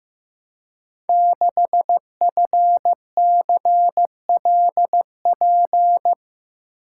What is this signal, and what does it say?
Morse code, heard as a keyed tone, about 15 wpm, 700 Hz. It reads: 6FCLP
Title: Code Groups - 5 Letter:Numbers - 15wpm